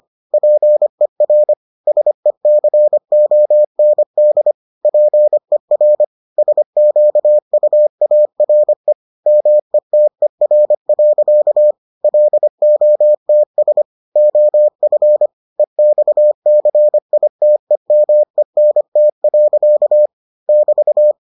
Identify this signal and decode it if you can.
Morse code, heard as a keyed tone, about 25 wpm, 600 Hz. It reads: PER SECOND PER SQUARE METER. LOTS OF EXCITEMENT. <BT>